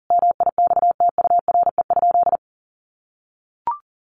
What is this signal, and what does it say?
Morse code, heard as a keyed tone, about 40 wpm, 700 Hz. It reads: MIXTURE?
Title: Sentences from Top 1500 Words 40wpm